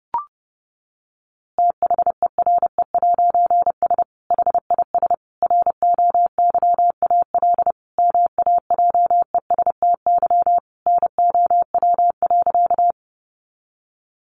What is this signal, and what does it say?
Morse code, heard as a keyed tone, about 30 wpm, 700 Hz. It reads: THERE'S HIS ROYAL MAJESTY NOW.
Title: A Sound Of Thunder 30wpm